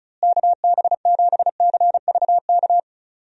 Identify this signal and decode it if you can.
Morse code, heard as a keyed tone, about 36 wpm, 700 Hz. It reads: KB7CVK
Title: USA Call Signs - 6 Characters - Repeated 2x - 35wpm